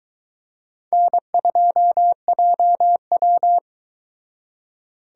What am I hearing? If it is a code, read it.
Morse code, heard as a keyed tone, about 22 wpm, 700 Hz. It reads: N2JW